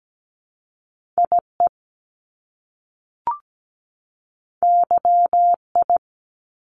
Morse code, heard as a keyed tone, about 17 wpm, 700 Hz. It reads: IE YI
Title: Top Two Letter Combinations in Words 17wpm